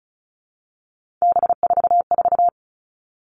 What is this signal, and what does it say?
Morse code, heard as a keyed tone, about 34 wpm, 700 Hz. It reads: B44
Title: All Cut Numbers - 3-Digit 35wpm